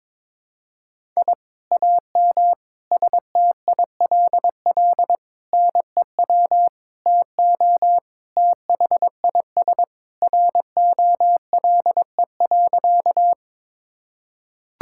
Morse code, heard as a keyed tone, about 22 wpm, 700 Hz. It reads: I AM STILL NEW TO THIS ROLE.
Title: Sentences from Top 1000 Words - Review Part 6 - 22wpm